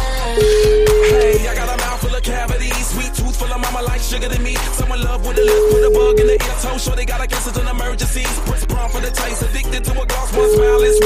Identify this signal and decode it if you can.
Ringback tone (Europe, 425 Hz), with noise.